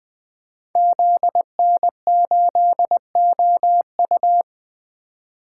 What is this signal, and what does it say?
Morse code, heard as a keyed tone, about 20 wpm, 700 Hz. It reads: ZN8OU
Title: Code Groups - 5 Letter:Numbers - 20wpm